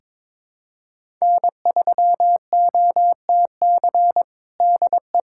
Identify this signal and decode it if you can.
Morse code, heard as a keyed tone, about 22 wpm, 700 Hz. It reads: N3OTC DE
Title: USA Callsign DE Callsign - 5 Characters - 22wpm